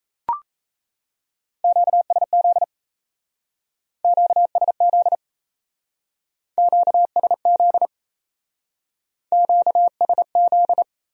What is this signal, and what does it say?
Morse code, heard as a keyed tone, about 34 wpm, 700 Hz. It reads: QSZ QSZ QSZ QSZ
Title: Speed-Racing Q-Codes - 28wpm